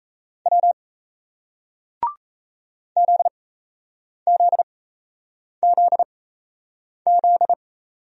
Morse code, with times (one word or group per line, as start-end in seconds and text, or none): 0.5-0.7 W
3.0-7.5 ZZZZ